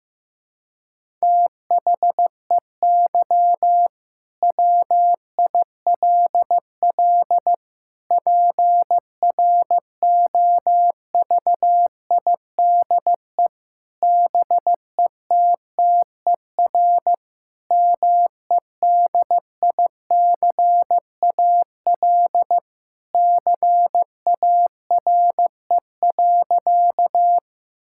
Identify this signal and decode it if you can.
Morse code, heard as a keyed tone, about 15 wpm, 700 Hz. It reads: THEY WILL PROVIDE BETTER MEDICAL CARE.
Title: Sentences from Top 1000 Words - Review Part 5 - 15wpm